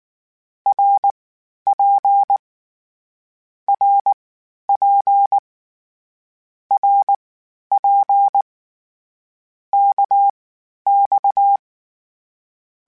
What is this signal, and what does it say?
Morse code, heard as a keyed tone, about 19 wpm, spaced as for 11 wpm, 800 Hz. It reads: RP RP RP KX
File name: Short Runs / Short Runs-mp3s / Balanced @18